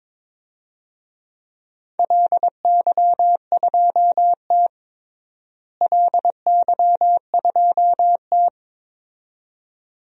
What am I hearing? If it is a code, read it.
Morse code, heard as a keyed tone, about 22 wpm, 700 Hz. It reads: LY2T LY2T
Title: DXCC-100 Call Signs - 4 Characters - Repeated 2x - 22wpm